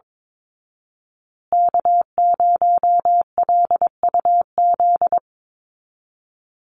Morse code, heard as a keyed tone, about 22 wpm, 700 Hz. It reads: K0LUZ
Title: Call Signs Repeated 2x - 22wpm